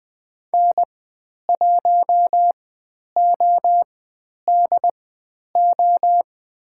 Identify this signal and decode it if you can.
Morse code, heard as a keyed tone, about 20 wpm, spaced as for 10 wpm, 700 Hz. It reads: N1ODO
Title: 095 Lesson 16g - Letter c - Call Signs 20wpm